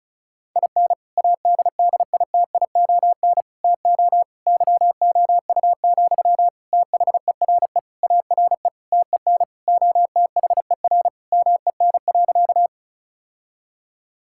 Morse code, heard as a keyed tone, about 36 wpm, 700 Hz. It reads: IN ADDITION TO YOU, THERE ARE TEN OTHER MEN.
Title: Sentences from Top 1000 Words - Review Part 6 - 35wpm